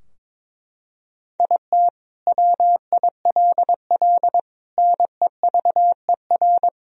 Morse code, heard as a keyed tone, about 22 wpm, 700 Hz. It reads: IT WILL NEVER
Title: Sentences from Top 1000 Words - Review Part 2 - 22wpm